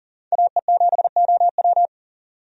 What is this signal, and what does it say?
Morse code, heard as a keyed tone, about 39 wpm, 700 Hz. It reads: AE7OW
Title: USA Call Signs - 5 Characters - 40wpm